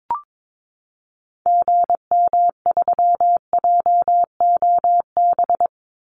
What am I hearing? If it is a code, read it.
Morse code, heard as a keyed tone, about 22 wpm, 700 Hz. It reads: GM3JOB
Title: Call Signs Repeated 2x - 22wpm